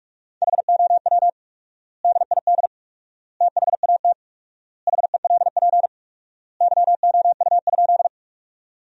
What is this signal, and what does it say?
Morse code, heard as a keyed tone, about 46 wpm, 700 Hz. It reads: HOW DID THAT HELP YOU?
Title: 4x Word-Spacing - Sentences from Top 200 Words - 45wpm